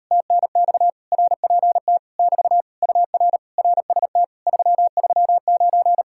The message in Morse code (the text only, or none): TNX RPT <BT> UR RST 339